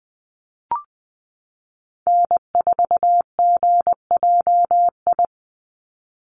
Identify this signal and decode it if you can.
Morse code, heard as a keyed tone, about 20 wpm, 700 Hz. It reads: N4GJI
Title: USA Call Signs 20wpm